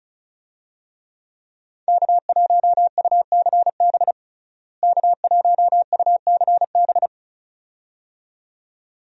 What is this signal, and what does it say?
Morse code, heard as a keyed tone, about 34 wpm, 700 Hz. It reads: K1UCB K1UCB
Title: USA Call Signs - 5 Characters - Repeated 2x - 35wpm